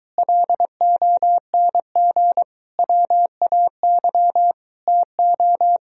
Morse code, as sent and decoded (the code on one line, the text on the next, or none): .-.. --- -. --. / .-- .- -.-- / - ---
LONG WAY TO